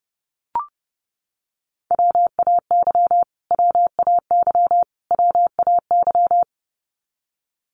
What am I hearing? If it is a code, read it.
Morse code, heard as a keyed tone, about 30 wpm, 700 Hz. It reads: WAY WAY WAY
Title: Top 500 Words Repeated 30wpm